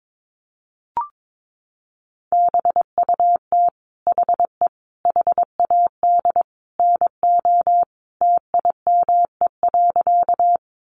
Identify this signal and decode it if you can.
Morse code, heard as a keyed tone, about 22 wpm, 700 Hz. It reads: BUT HE HAD NO TIME.
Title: Sentences from Top 100 Words 22wpm